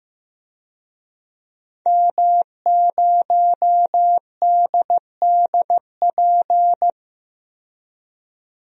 Morse code, heard as a keyed tone, about 15 wpm, 700 Hz. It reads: M0DDP
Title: DXCC-100 Call Signs - 5 Characters - 15wpm